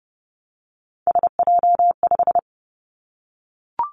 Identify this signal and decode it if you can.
Morse code, heard as a keyed tone, about 30 wpm, 700 Hz. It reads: SJ5
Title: Three Challenging Letter-Number Combinations 30wpm